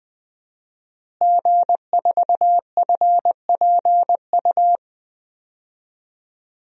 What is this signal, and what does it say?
Morse code, heard as a keyed tone, about 20 wpm, 700 Hz. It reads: G4FPU
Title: DXCC-100 Call Signs - 5 Characters - Repeated 2x - 20wpm